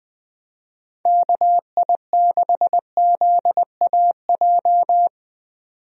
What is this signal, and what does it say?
Morse code, heard as a keyed tone, about 20 wpm, 700 Hz. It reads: KI6ZAJ
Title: USA Call Signs - 6 Characters - 20wpm